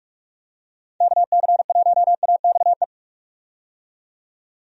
Morse code, heard as a keyed tone, about 46 wpm, 700 Hz. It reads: KC1AXE
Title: USA Call Signs - 6 Characters - 45wpm